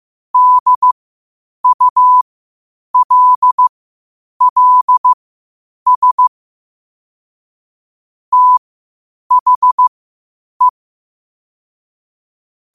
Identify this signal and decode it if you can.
Morse code, heard as a keyed tone, about 15 wpm, spaced as for 8 wpm, 1000 Hz. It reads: DULLS THE
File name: qotdfiles / 8wpm